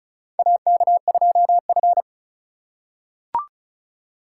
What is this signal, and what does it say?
Morse code, heard as a keyed tone, about 36 wpm, 700 Hz. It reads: AK2F
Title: USA Call Signs - 4 Characters - 6x Character Spacing - 35wpm